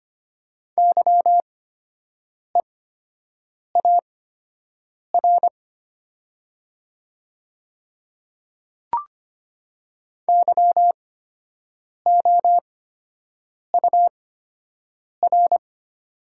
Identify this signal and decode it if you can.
Morse code, heard as a keyed tone, about 25 wpm, spaced as for 7 wpm, 700 Hz. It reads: YEAR YOUR
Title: 8x Character Spacing - Top 200 Words - 25wpm